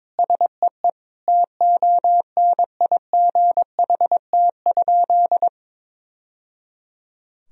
Morse code, heard as a keyed tone, about 22 wpm, 700 Hz. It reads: SEE TONIGHT?
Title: Sentences from Top 1000 Words - Review Part 2 - 22wpm